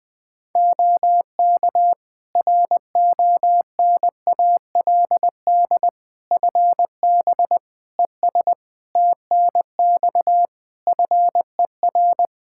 Morse code, heard as a keyed tone, about 20 wpm, 700 Hz. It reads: OK RONALD FB ES TNX FER
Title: Practice QSO - Exchange 2 - Info - 20wpm